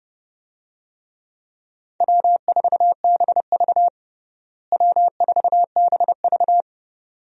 Morse code, heard as a keyed tone, about 30 wpm, 700 Hz. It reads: W4BV W4BV
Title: USA Call Signs - 4 Characters - Repeated 2x - 30wpm